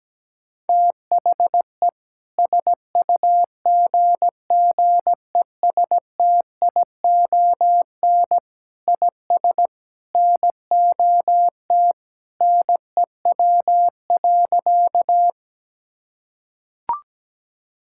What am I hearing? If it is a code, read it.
Morse code, heard as a keyed tone, about 17 wpm, 700 Hz. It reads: THE SUGGESTION IS NOT NEW.
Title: Sentences from Top 1600 Words 17wpm